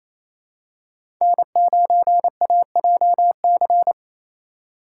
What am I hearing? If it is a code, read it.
Morse code, heard as a keyed tone, about 28 wpm, 700 Hz. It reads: N9AJC
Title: USA Call Signs - 3x Character Spacing - 28wpm